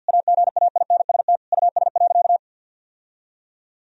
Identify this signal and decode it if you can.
Morse code, heard as a keyed tone, about 46 wpm, 700 Hz. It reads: AGAINST US.
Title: Sentences from Top 1000 Words - Review Part 2 - 50wpm